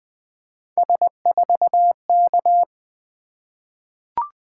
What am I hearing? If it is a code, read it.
Morse code, heard as a keyed tone, about 20 wpm, 700 Hz. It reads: S4K
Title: Three Letter-Number Combinations - 20wpm